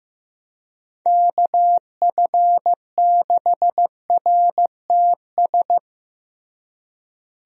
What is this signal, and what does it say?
Morse code, heard as a keyed tone, about 15 wpm, 700 Hz. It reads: KF6RTS
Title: USA Call Signs - 6 Characters - Repeated 2x - 15wpm